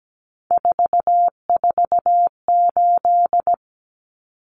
Morse code, heard as a keyed tone, about 17 wpm, 700 Hz. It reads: 448
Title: Common Cut Numbers - 3-Digit 17wpm